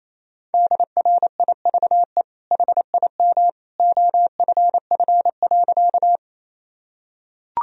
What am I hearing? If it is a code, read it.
Morse code, heard as a keyed tone, about 28 wpm, 700 Hz. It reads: DRIVE HIM OFF.
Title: Sentences from Top 1000 Words - Review Part 2 - 28wpm